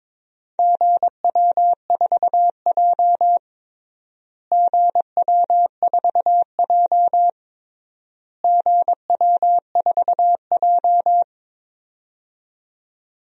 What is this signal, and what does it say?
Morse code, heard as a keyed tone, about 22 wpm, 700 Hz. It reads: GW4J GW4J GW4J
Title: DXCC-100 Call Signs - 4 Characters - Repeated 3x - 22wpm